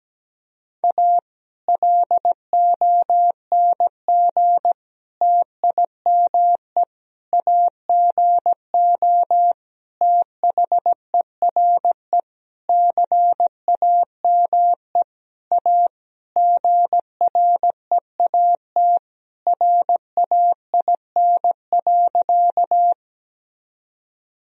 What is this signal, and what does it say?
Morse code, heard as a keyed tone, about 17 wpm, 700 Hz. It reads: A LONG TIME AGO THERE CAME A GREAT RAIN.
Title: Sentences from Top 1000 Words - Review Part 3 - 17wpm